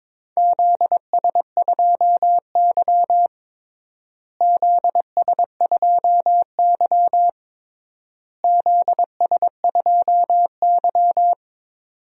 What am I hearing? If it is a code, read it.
Morse code, heard as a keyed tone, about 22 wpm, 700 Hz. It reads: ZS2Y ZS2Y ZS2Y
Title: DXCC-100 Call Signs - 4 Characters - Repeated 3x - 22wpm